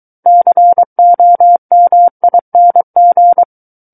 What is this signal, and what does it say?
Morse code, heard as a keyed tone, about 23 wpm, 700 Hz. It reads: COMING
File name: Book One / 14-Chapter 14 - 23 WPM